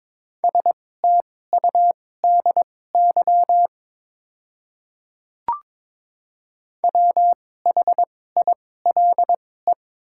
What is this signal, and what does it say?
Morse code, heard as a keyed tone, about 22 wpm, spaced as for 16 wpm, 700 Hz. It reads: STUDY WHILE
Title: 2x Character Spacing - Top 300 Words - 22wpm